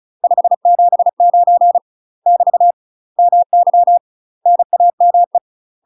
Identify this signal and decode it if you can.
Morse code, heard as a keyed tone, about 36 wpm, 700 Hz. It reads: 579 <BT> MY NAME
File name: UK MASTER QSOs TRAINING/35WPM